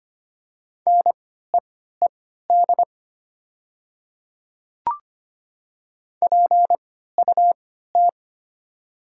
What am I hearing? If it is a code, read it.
Morse code, heard as a keyed tone, about 25 wpm, spaced as for 14 wpm, 700 Hz. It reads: NEED PUT
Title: 3x Character Spacing - Top 200 Words - 25wpm